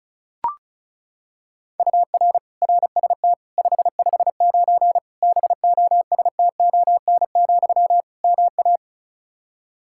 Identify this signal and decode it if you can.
Morse code, heard as a keyed tone, about 36 wpm, 700 Hz. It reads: UR RST 559 BOSTON, MA
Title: K1USN SST QSO Practice 35wpm